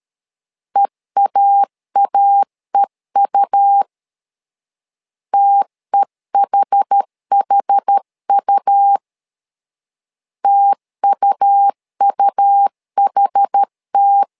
Morse code, heard as a keyed tone, about 13 wpm, 800 Hz. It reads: EAAEU TEHHU TUUHT
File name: LEZIONI / 2 UHDN / 2 UHD 12wpm 1540